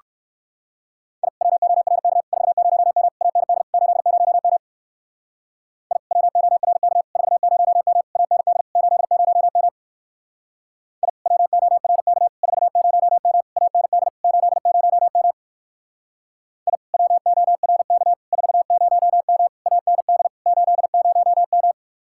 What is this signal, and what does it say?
Morse code, heard as a keyed tone, about 58 wpm, 700 Hz. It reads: I WORK 40M AND 80M I WORK 40M AND 80M I WORK 40M AND 80M I WORK 40M AND 80M